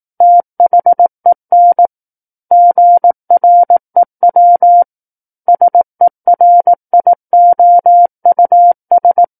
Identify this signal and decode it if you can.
Morse code, heard as a keyed tone, about 18 wpm, 700 Hz. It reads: THEN GREW SERIOUS
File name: Book One / 09-Chapter 9 - 18 WPM